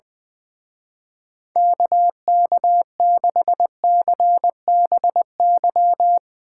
Morse code, heard as a keyed tone, about 21 wpm, 700 Hz. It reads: KK6CBY
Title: Speed Racing - USA 2x3 Call Signs 20wpm